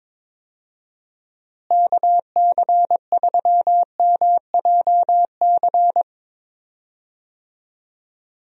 Morse code, heard as a keyed tone, about 22 wpm, 700 Hz. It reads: KC3MJC
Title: USA Call Signs - 6 Characters - Repeated 3x - 22wpm